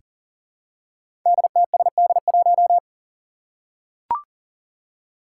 Morse code, heard as a keyed tone, about 40 wpm, 700 Hz. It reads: DTSD1
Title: Five Challenging Letter-Number Combinations 40wpm